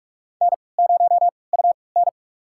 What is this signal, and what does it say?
Morse code, heard as a keyed tone, about 41 wpm, spaced as for 24 wpm, 700 Hz. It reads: N0UN
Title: USA Call Signs - 4 Characters - 3x Character Spacing - 45wpm